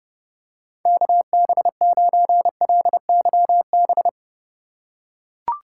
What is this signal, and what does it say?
Morse code, heard as a keyed tone, about 30 wpm, 700 Hz. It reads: KB9LYB
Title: USA Call Signs - 6 Characters - 3x Character Spacing - 30wpm